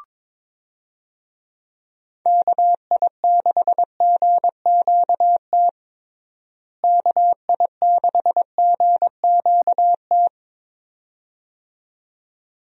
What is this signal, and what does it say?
Morse code, heard as a keyed tone, about 22 wpm, 700 Hz. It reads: KI6GQT KI6GQT
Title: USA Call Signs - 6 Characters - Repeated 2x - 22wpm